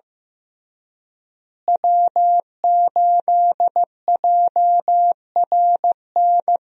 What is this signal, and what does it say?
Morse code, heard as a keyed tone, about 16 wpm, 700 Hz. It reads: W8JRN
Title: USA Call Signs - 5 Characters - Repeated 3x - 15wpm